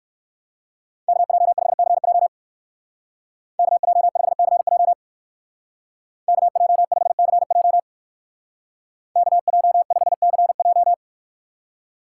Morse code, heard as a keyed tone, about 58 wpm, 700 Hz. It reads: KJ5CJ KJ5CJ KJ5CJ KJ5CJ